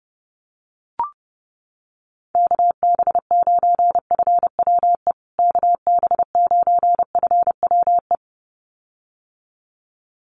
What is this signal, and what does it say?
Morse code, heard as a keyed tone, about 30 wpm, 700 Hz. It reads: KB9FWE KB9FWE
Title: USA Call Signs Repeated 2x 30wpm